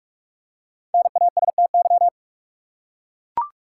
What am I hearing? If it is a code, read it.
Morse code, heard as a keyed tone, about 41 wpm, 700 Hz. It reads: NASTY
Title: Five Letter Words - 45wpm